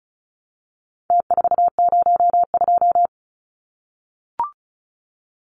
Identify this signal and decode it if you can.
Morse code, heard as a keyed tone, about 34 wpm, 700 Hz. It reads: T402
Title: Numbers - Contest Style - 4-Digit 35wpm